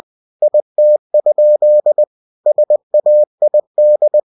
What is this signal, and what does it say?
Morse code, heard as a keyed tone, about 20 wpm, 600 Hz. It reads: IT? SAID
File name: cw-books / winnie-the-pooh / winnie-20wpm-page-0058